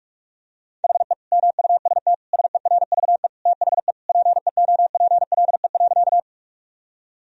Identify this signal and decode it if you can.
Morse code, heard as a keyed tone, about 41 wpm, 700 Hz. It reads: HE MUST SERVE THE PEOPLE.
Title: Sentences from Top 1000 Words - Review Part 2 - 45wpm